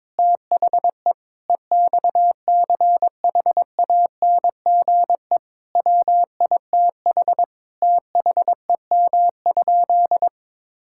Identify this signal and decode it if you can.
Morse code, heard as a keyed tone, about 22 wpm, 700 Hz. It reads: THE EXCHANGE WITH THEM?
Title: Sentences from Top 1000 Words - Review Part 6 - 22wpm